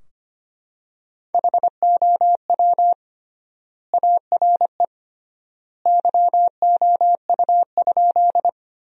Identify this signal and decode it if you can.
Morse code, heard as a keyed tone, about 25 wpm, 700 Hz. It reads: HOW ARE YOU?
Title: 3x Word-Spacing - Sentences from Top 100 Words - 25wpm